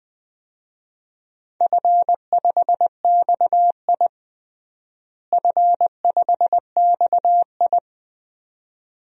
Morse code, heard as a keyed tone, about 20 wpm, 700 Hz. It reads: F5XI F5XI
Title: DXCC-100 Call Signs - 4 Characters - Repeated 2x - 20wpm